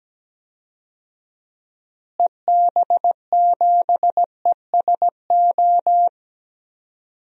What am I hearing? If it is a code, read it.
Morse code, heard as a keyed tone, about 17 wpm, 700 Hz. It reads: EB7ESO
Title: DXCC-100 Call Signs - 6 Characters - Repeated 2x - 17wpm